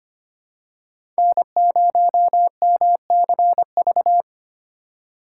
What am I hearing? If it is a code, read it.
Morse code, heard as a keyed tone, about 25 wpm, 700 Hz. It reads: N0MCV
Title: USA Call Signs - 5 Characters - 2x Character Spacing - 25wpm